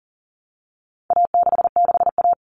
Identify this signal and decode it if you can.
Morse code, heard as a keyed tone, about 40 wpm, 700 Hz. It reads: A66A
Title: All Cut Numbers - 4-Digit 40wpm